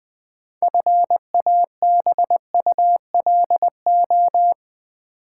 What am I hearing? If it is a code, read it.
Morse code, heard as a keyed tone, about 20 wpm, 700 Hz. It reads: FABULO
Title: Build A Word - 3 Letters At A Time - 8-Letter Words - 20wpm